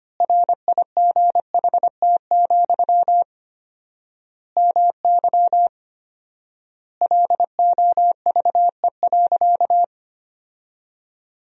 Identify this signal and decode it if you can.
Morse code, heard as a keyed tone, about 25 wpm, 700 Hz. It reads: RIGHT, MY LOVE.
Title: 4x Word-Spacing - Sentences from Top 400 Words - 25wpm